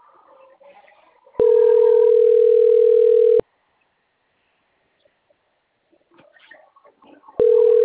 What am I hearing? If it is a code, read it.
Ringback tone (North America), 440 + 480 Hz.